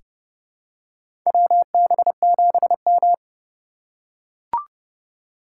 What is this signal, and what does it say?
Morse code, heard as a keyed tone, about 30 wpm, 700 Hz. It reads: WB7M
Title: USA Call Signs - 3x Character Spacing - 30wpm